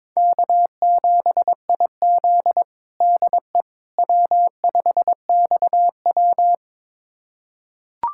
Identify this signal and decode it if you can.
Morse code, heard as a keyed tone, about 22 wpm, 700 Hz. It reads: K7IZ DE W5XW
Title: USA Callsign DE Callsign - 4 Characters - 22wpm